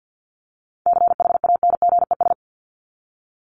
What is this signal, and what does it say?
Morse code, heard as a keyed tone, about 50 wpm, 700 Hz. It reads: CHANGES
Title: ES:S Suffix - Top 100 Words with Base Words 50wpm